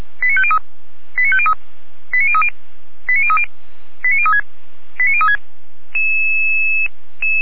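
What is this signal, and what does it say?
Five-tone selective call (ZVEI), heard as quick runs of five tones, tones of about 70 ms. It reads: 89503 89503 89030 89030 89036 89036
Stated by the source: ZVEI five-tone selective call received by SDR, Feuerwehr Orlamünde audio server.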